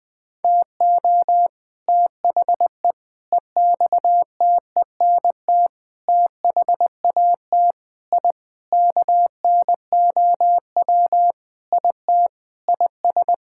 Morse code, heard as a keyed tone, about 21 wpm, 700 Hz. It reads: TO THE EXTENT THAT I KNOW IT IS
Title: Sentences with 4-Word Phrases - 20wpm